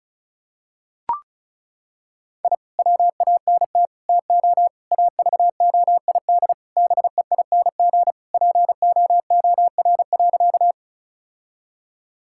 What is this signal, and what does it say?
Morse code, heard as a keyed tone, about 34 wpm, 700 Hz. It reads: I WANT TO AVOID BEING POOR.
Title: Sentences from Top 1000 Words - Review Part 6 - 35wpm